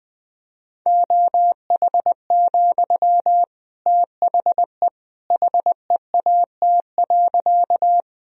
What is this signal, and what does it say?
Morse code, heard as a keyed tone, about 20 wpm, 700 Hz. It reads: OH, THE HEAT.
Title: Sentences from Top 1000 Words - Review Part 3 - 20wpm